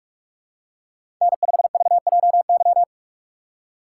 Morse code, heard as a keyed tone, about 44 wpm, 700 Hz. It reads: N5VJY
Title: USA Call Signs - 3x Character Spacing - 45wpm